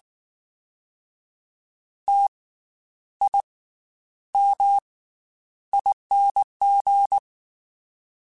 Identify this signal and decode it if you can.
Morse code, heard as a keyed tone, about 19 wpm, 800 Hz. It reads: T I M ING
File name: Special INGs / Special INGs-mp3s / Slow ing @18